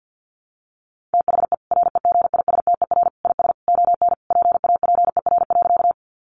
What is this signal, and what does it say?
Morse code, heard as a keyed tone, about 50 wpm, 700 Hz. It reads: THE REGISTER IS ON PAPER.
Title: Sentences from Top 1200 Words 50wpm